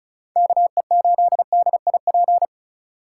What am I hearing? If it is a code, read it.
Morse code, heard as a keyed tone, about 45 wpm, 700 Hz. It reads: KE8DIP